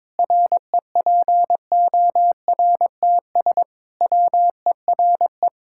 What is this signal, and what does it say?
Morse code, heard as a keyed tone, about 22 wpm, 700 Hz. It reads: REPORTS WERE
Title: Pangram Challenge v2 - 22wpm